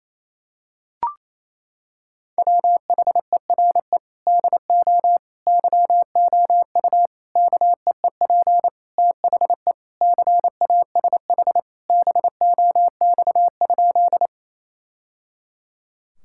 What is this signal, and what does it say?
Morse code, heard as a keyed tone, about 27 wpm, 700 Hz. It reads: WHERE DO YOU KEEP THE CASH BOX?
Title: Sentences from Top 1000 Words - Review Part 5 - 28wpm